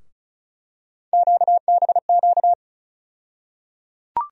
Morse code, heard as a keyed tone, about 36 wpm, 700 Hz. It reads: QBQ
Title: USA Call Sign Postfixes - 3 Letters 35wpm